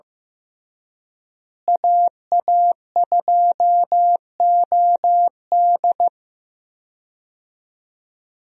Morse code, heard as a keyed tone, about 15 wpm, 700 Hz. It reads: AA2OD
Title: USA Call Signs - 5 Characters - Repeated 3x - 15wpm